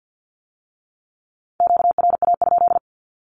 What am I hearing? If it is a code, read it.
Morse code, heard as a keyed tone, about 50 wpm, 700 Hz. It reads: QRA?
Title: Q-Codes with Spoken Meaning - 50wpm